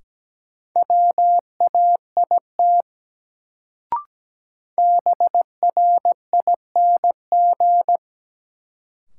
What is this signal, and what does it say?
Morse code, heard as a keyed tone, about 17 wpm, 700 Hz. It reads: WAIT BRING
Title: Sentences from Top 500 Words - New Word Drill - 17wpm